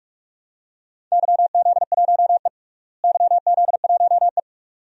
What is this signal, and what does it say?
Morse code, heard as a keyed tone, about 41 wpm, 700 Hz. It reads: YZ1E YZ1E
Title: DXCC-100 Call Signs - 4 Characters - Repeated 2x - 45wpm